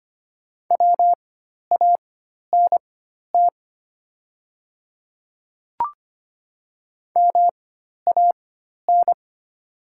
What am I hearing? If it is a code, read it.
Morse code, heard as a keyed tone, about 25 wpm, spaced as for 12 wpm, 700 Hz. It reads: WANT MAN
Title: 4x Character Spacing - Top 200 Words - 25wpm